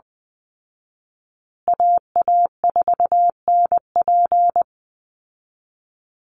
Morse code, heard as a keyed tone, about 19 wpm, 700 Hz. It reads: AA4NP
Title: Call Signs Repeated 2x - 20wpm